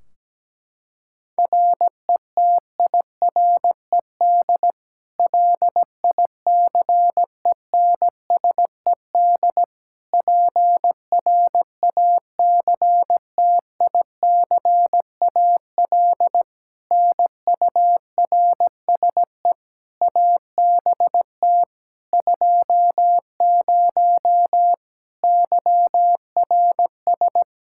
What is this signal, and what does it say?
Morse code, heard as a keyed tone, about 17 wpm, 700 Hz. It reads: RETIRED LICENSED PRACTICAL NURSE ABT 20 YRS
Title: Practice QSO - Exchange 3 - More Info - 17wpm